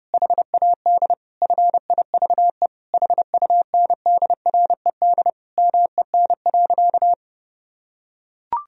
Morse code, heard as a keyed tone, about 32 wpm, 700 Hz. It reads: HAD FIVE HUNDRED MEN.
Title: Sentences from Top 1000 Words - Review Part 2 - 30wpm